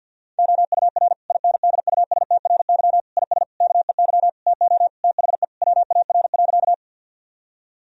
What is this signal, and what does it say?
Morse code, heard as a keyed tone, about 50 wpm, 700 Hz. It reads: OUR INDUSTRY IS KEY TO THE WAR.
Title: Sentences from Top 1000 Words - Review Part 4 - 50wpm